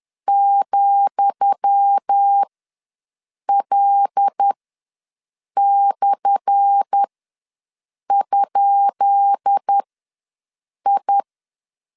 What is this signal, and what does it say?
Morse code, heard as a keyed tone, about 10 wpm, spaced as for 6 wpm, 800 Hz. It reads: ,L/?I
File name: LEZIONI / 10 SIGNS / 10D COMA ws 10wpm 4314